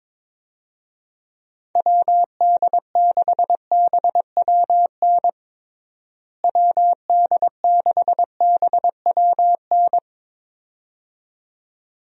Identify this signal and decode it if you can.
Morse code, heard as a keyed tone, about 22 wpm, 700 Hz. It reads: WD6BWN WD6BWN
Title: USA Call Signs - 6 Characters - Repeated 2x - 22wpm